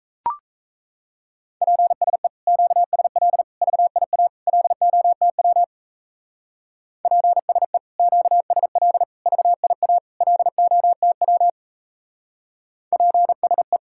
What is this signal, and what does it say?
Morse code, heard as a keyed tone, about 36 wpm, 700 Hz. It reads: PSE QSL VIA LOTW PSE QSL VIA LOTW PSE